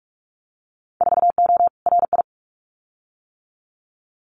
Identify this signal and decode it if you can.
Morse code, heard as a keyed tone, about 46 wpm, 700 Hz. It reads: 4O RI
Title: Winter Field Day Exchanges Encore 45wpm